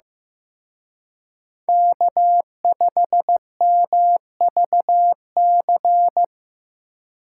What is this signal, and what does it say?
Morse code, heard as a keyed tone, about 18 wpm, 700 Hz. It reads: K5MVC
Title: Speed Racing - USA Call Signs 15wpm